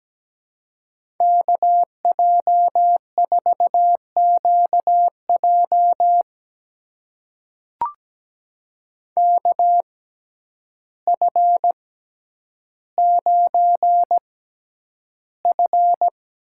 Morse code, heard as a keyed tone, about 17 wpm, spaced as for 9 wpm, 700 Hz. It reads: KJ4QJ KF9F
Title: USA Call Signs - 5 Characters - 6x Character Spacing - 17wpm